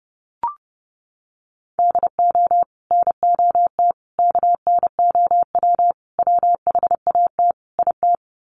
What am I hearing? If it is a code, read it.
Morse code, heard as a keyed tone, about 30 wpm, 700 Hz. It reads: DO NOT KNOW WHAT IT
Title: Sets of 5 Words 30wpm